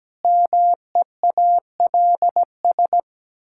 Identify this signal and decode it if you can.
Morse code, heard as a keyed tone, about 17 wpm, 700 Hz. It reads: MEALS
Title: Pangram Challenge v2 - 17wpm